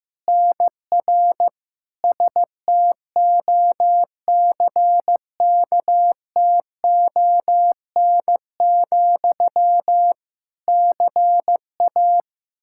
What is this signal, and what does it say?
Morse code, heard as a keyed tone, about 15 wpm, 700 Hz. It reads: NR STOCKTON, CA
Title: Practice QSO - Exchange 1 - Beginning - 15wpm